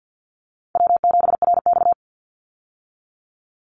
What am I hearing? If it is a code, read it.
Morse code, heard as a keyed tone, about 50 wpm, 700 Hz. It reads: W7RX
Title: Call Signs - 50wpm